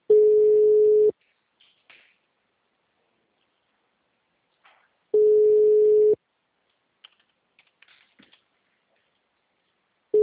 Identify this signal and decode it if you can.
Ringback tone (Europe, 425 Hz).